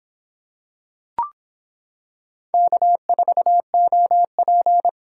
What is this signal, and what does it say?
Morse code, heard as a keyed tone, about 25 wpm, 700 Hz. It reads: K4OP